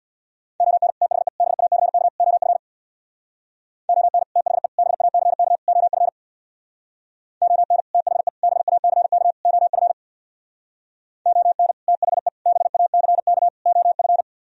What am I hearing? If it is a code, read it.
Morse code, heard as a keyed tone, about 63 wpm, 690 Hz. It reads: ON THE BACK OF ON THE BACK OF ON THE BACK OF ON THE BACK OF